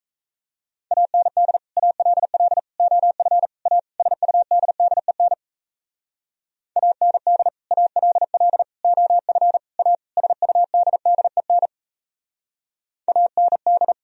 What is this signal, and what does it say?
Morse code, heard as a keyed tone, about 34 wpm, 700 Hz. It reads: AND ALL OF A SUDDEN AND ALL OF A SUDDEN AND
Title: Speed-Racing - Sets of 5 Words 28wpm